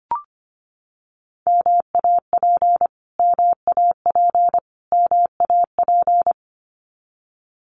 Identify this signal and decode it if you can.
Morse code, heard as a keyed tone, about 25 wpm, 700 Hz. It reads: MAP MAP MAP
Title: Top 500 Words Repeated 25wpm